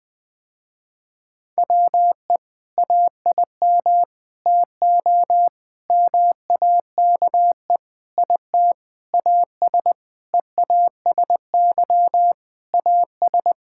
Morse code, heard as a keyed tone, about 20 wpm, 700 Hz. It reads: WE AIM TO MAKE IT AS EASY AS
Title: Sentences with 7-Word Phrases - Spoken Beginning - 20wpm